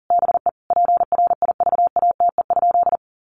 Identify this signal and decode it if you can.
Morse code, heard as a keyed tone, about 40 wpm, 700 Hz. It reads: BE PRIVATE?
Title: Sentences from Top 1000 Words 40wpm